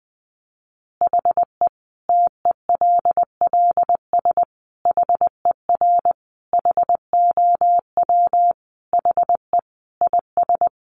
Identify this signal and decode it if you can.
Morse code, heard as a keyed tone, about 20 wpm, 700 Hz. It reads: HE TELLS HER HOW HE IS
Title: Additional Easy Phrases 20wpm